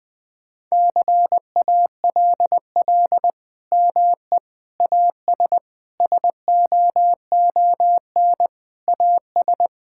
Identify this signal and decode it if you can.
Morse code, heard as a keyed tone, about 20 wpm, 700 Hz. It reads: CALL ME AS SOON AS
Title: Sentences with 3-Word Phrases - Spoken Beginning - 20wpm